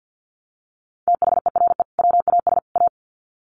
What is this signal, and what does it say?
Morse code, heard as a keyed tone, about 50 wpm, 700 Hz. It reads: THERE WAS A
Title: Sets of 3 Words 50wpm